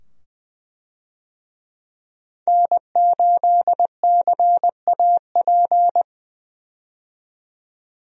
Morse code, heard as a keyed tone, about 20 wpm, 700 Hz. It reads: N8CAP
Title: USA Call Signs - 5 Characters - 20wpm